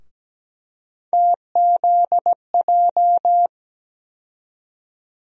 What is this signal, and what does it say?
Morse code, heard as a keyed tone, about 17 wpm, 700 Hz. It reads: TZJ
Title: USA Call Sign Postfixes - 3 Letters 17wpm